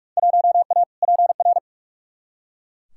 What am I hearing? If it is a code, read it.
Morse code, heard as a keyed tone, about 43 wpm, 700 Hz. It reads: 1A PR
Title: Field Day Exchanges - 45wpm